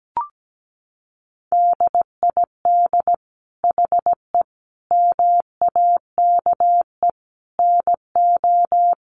Morse code, heard as a keyed tone, about 17 wpm, 700 Hz. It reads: DID HE MAKE NO
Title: Sentences from Top 1000 Words 17wpm